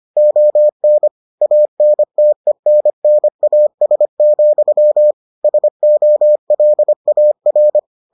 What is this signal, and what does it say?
Morse code, heard as a keyed tone, about 25 wpm, 600 Hz. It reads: ON ANTENNAS, SOLAR